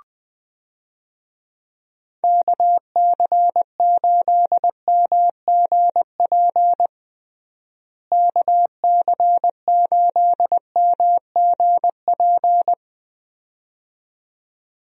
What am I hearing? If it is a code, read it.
Morse code, heard as a keyed tone, about 20 wpm, 700 Hz. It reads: KC8MGP KC8MGP
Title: USA Call Signs - 6 Characters - Repeated 2x - 20wpm